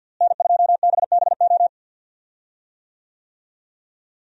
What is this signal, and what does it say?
Morse code, heard as a keyed tone, about 49 wpm, 700 Hz. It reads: N2BBO
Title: USA Call Signs - 5 Characters - Repeated 2x - 50wpm